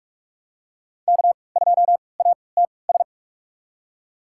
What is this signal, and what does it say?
Morse code, heard as a keyed tone, about 46 wpm, spaced as for 25 wpm, 700 Hz. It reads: K2ATS
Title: USA Call Signs - 3x Character Spacing - 45wpm